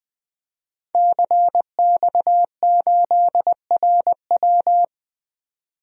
Morse code, heard as a keyed tone, about 20 wpm, 700 Hz. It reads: CX8RW
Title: DXCC-100 Call Signs - 5 Characters - 20wpm